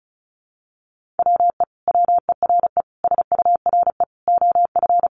Morse code, heard as a keyed tone, about 34 wpm, 700 Hz. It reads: WE WERE SURE OF
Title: Sentences from Top 1100 Words 35wpm